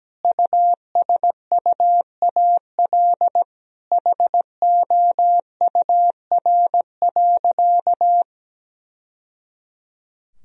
Morse code, heard as a keyed tone, about 17 wpm, 700 Hz. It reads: USUAL HOUR.
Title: Sentences from Top 1000 Words - Review Part 2 - 17wpm